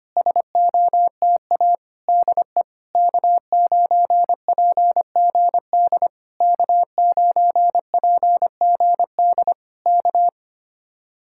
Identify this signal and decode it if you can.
Morse code, heard as a keyed tone, about 25 wpm, 700 Hz. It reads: SOTA DE K9PGD K9PGD K
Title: Speed-Racing Common QSO Phrases - 25wpm